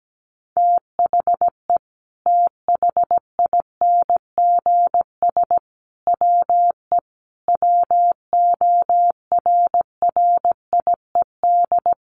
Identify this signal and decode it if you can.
Morse code, heard as a keyed tone, about 17 wpm, 700 Hz. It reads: THE THINGS WE WORRIED
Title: A Sound Of Thunder 17wpm